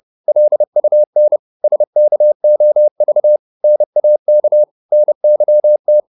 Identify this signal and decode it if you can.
Morse code, heard as a keyed tone, about 30 wpm, 600 Hz. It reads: LUN SKOV NAK NYT
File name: Korte ord DK 30wpm